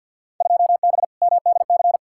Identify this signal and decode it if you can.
Morse code, heard as a keyed tone, about 53 wpm, 700 Hz. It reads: 2B MDC
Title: Field Day Exchanges - 50wpm